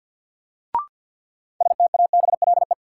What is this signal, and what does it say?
Morse code, heard as a keyed tone, about 49 wpm, 700 Hz. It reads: STABLE
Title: Sentences from Top 1500 Words - New Word Drill - 50wpm